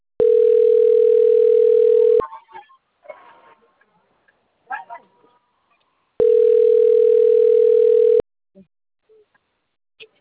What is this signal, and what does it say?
Ringback tone (North America), 440 + 480 Hz, with speech.
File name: rg-1600-unknown- - - .828